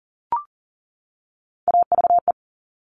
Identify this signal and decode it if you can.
Morse code, heard as a keyed tone, about 40 wpm, 700 Hz. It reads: AVE
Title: Top Three Letter Combinations in Words 40wpm